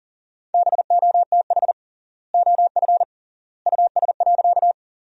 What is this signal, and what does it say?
Morse code, heard as a keyed tone, about 40 wpm, 700 Hz. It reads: BOTH OF US.
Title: 3x Word-Spacing - Sentences from Top 300 Words - 40wpm